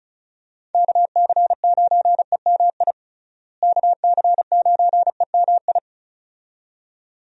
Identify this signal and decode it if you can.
Morse code, heard as a keyed tone, about 36 wpm, 700 Hz. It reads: KC9EMI KC9EMI
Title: USA Call Signs - 6 Characters - Repeated 2x - 35wpm